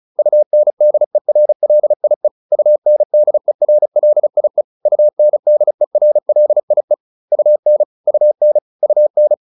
Morse code, heard as a keyed tone, about 34 wpm, 600 Hz. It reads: UNDERLIE UNDERLIE UNDERLIE UN UN UN
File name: UN8 35